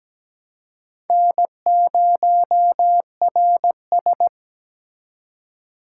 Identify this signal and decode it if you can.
Morse code, heard as a keyed tone, about 20 wpm, 700 Hz. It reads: N0RS
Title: Speed Racing - USA 1x2 Call Signs 17wpm